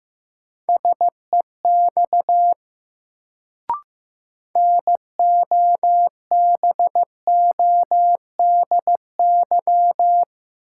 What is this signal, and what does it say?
Morse code, heard as a keyed tone, about 15 wpm, 700 Hz. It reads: SEX NOBODY
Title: Sentences from Top 1600 Words - New Word Drill - 15wpm